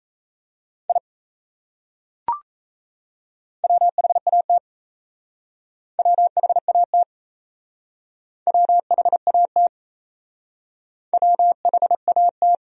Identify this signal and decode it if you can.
Morse code, heard as a keyed tone, about 35 wpm, 700 Hz. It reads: I WHAT WHAT WHAT WHAT
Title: Speed-Racing Top 100 Words 28wpm